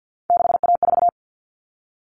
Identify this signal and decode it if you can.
Morse code, heard as a keyed tone, about 50 wpm, 700 Hz. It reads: 6A4
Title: Common Cut Numbers - 3-Digit 50wpm